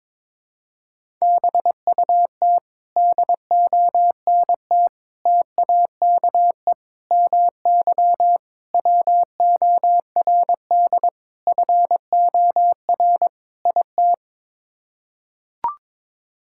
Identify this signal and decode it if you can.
Morse code, heard as a keyed tone, about 21 wpm, 700 Hz. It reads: BUT DONT TAKE MY WORD FOR IT
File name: practice / 22wpm / Sets of 7 Words - 22wpm